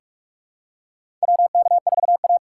Morse code, heard as a keyed tone, about 41 wpm, 700 Hz. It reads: WK4A
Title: USA Call Signs - 4x Character Spacing - 45wpm